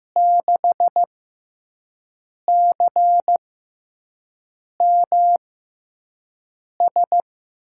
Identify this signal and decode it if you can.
Morse code, heard as a keyed tone, about 15 wpm, spaced as for 5 wpm, 700 Hz. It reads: 6CMS
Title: USA Call Signs - 6 Characters - 6x Character Spacing - 15wpm